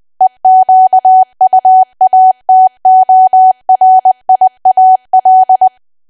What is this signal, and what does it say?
Morse code, heard as a keyed tone, about 20 wpm, 750 Hz. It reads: EQUATORIAL
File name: cwa / W1AW-Level3 / 20WPM